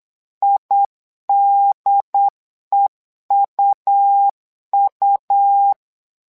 Morse code, heard as a keyed tone, about 8 wpm, 800 Hz. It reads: IDEUU
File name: LEZIONI / 3 MWRO / 3 MWRO 8wpm 2606